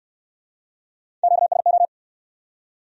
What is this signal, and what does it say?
Morse code, heard as a keyed tone, about 60 wpm, 700 Hz. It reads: QSO